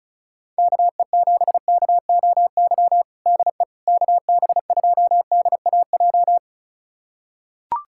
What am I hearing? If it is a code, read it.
Morse code, heard as a keyed tone, about 36 wpm, 690 Hz. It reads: KE7KOY DE KB2DAJ
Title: USA Callsign DE Callsign - 6 Characters - 35wpm